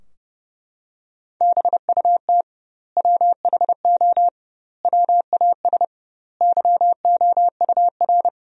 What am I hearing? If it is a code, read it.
Morse code, heard as a keyed tone, about 32 wpm, 700 Hz. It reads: BUT WHO WAS YOUR
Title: 2x Word-Spacing - Sentences from Top 200 Words - 30wpm